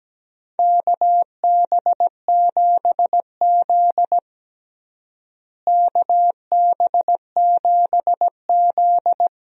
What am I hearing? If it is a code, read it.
Morse code, heard as a keyed tone, about 17 wpm, 700 Hz. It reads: KB7Z KB7Z
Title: USA Call Signs - 4 Characters - Repeated 2x - 17wpm